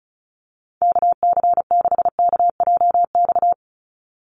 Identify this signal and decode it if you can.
Morse code, heard as a keyed tone, about 34 wpm, 700 Hz. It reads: KC6KJX
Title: USA Call Signs 35wpm